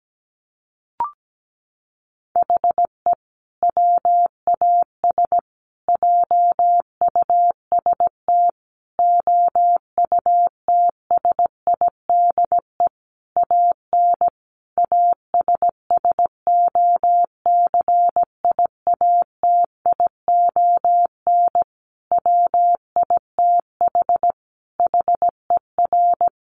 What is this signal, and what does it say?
Morse code, heard as a keyed tone, about 17 wpm, 700 Hz. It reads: HE WAS JUST OUTSIDE AN ASSOCIATION WITH HER
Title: Sentences from Top 1100 Words 17wpm